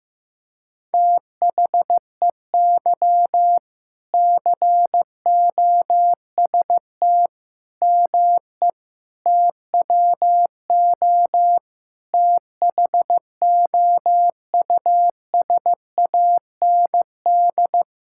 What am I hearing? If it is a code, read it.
Morse code, heard as a keyed tone, about 15 wpm, 700 Hz. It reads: THEY COST ME TWO THOUSAND
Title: Sentences from Top 1000 Words - Review Part 5 - 15wpm